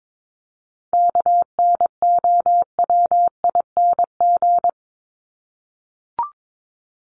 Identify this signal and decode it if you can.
Morse code, heard as a keyed tone, about 22 wpm, 700 Hz. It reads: KNOWING
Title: ING Suffix - Top 100 Words with Base Words 22wpm